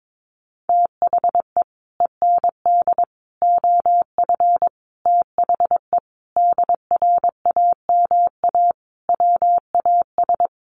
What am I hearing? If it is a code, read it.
Morse code, heard as a keyed tone, about 22 wpm, 700 Hz. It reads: THE END OF THE DRAMA WAS
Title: Sentences from Top 1500 Words 22wpm